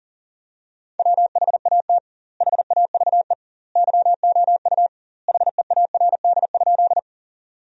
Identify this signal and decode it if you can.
Morse code, heard as a keyed tone, about 37 wpm, 700 Hz. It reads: WHAT HAVE YOU HEARD?
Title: 2x Word-Spacing - Sentences from Top 400 Words - 40wpm